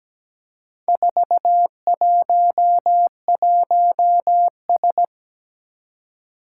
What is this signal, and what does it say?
Morse code, heard as a keyed tone, about 17 wpm, 700 Hz. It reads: 411S
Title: Four Challenging Letter-Number Combinations 17wpm